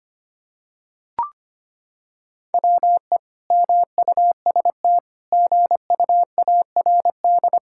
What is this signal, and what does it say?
Morse code, heard as a keyed tone, about 25 wpm, 700 Hz. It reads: WE MUST GUARD
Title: Sentences from Top 1600 Words 25wpm